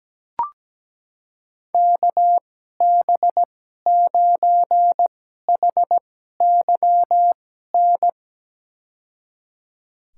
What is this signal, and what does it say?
Morse code, heard as a keyed tone, about 17 wpm, spaced as for 12 wpm, 700 Hz. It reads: KB9HYN
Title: USA Call Signs - 6 Characters - 2x Character Spacing - 17wpm